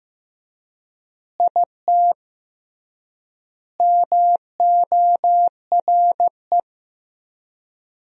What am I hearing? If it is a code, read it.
Morse code, heard as a keyed tone, about 15 wpm, 700 Hz. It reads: IT MORE
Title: 3x Word-Spacing - Sentences from Top 400 Words - 15wpm